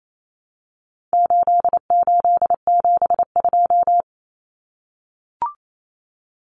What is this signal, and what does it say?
Morse code, heard as a keyed tone, about 28 wpm, 700 Hz. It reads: 8872
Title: Numbers - 4-Digit 28wpm